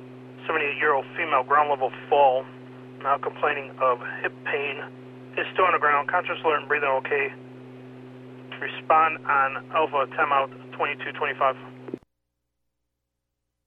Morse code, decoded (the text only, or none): none